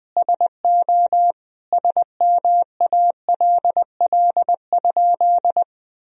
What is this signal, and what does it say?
Morse code, heard as a keyed tone, about 21 wpm, 700 Hz. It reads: SO SMALL?
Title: Sentences from Top 1000 Words - Review Part 1 - 20wpm